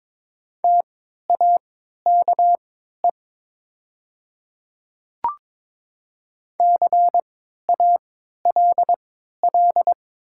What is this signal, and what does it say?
Morse code, heard as a keyed tone, about 23 wpm, spaced as for 13 wpm, 700 Hz. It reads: TAKE CALL
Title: 3x Character Spacing - Top 300 Words - 22wpm